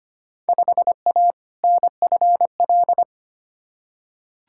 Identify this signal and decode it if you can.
Morse code, heard as a keyed tone, about 25 wpm, 700 Hz. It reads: 5A NFL
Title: Field Day Exchanges - 25wpm